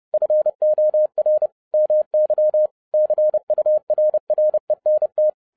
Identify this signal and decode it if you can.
Morse code, heard as a keyed tone, about 30 wpm, 610 Hz. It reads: FOR MY CURRENT
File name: 2012-01-mp3 / n11-I DIDNT MISS THE SOLAR CYCLE 24-30wpm